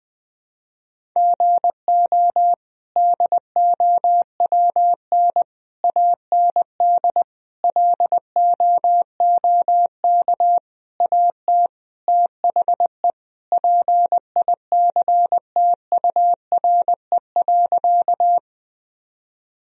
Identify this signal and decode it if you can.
Morse code, heard as a keyed tone, about 20 wpm, 700 Hz. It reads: GO DOWN AND LOOK AT THE PICTURE.
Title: Sentences from Top 1000 Words - Review Part 1 - 20wpm